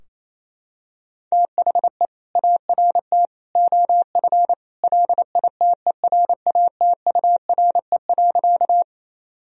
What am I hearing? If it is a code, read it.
Morse code, heard as a keyed tone, about 28 wpm, 700 Hz. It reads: THE ART OF LITERATURE.
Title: Sentences from Top 1000 Words - Review Part 3 - 28wpm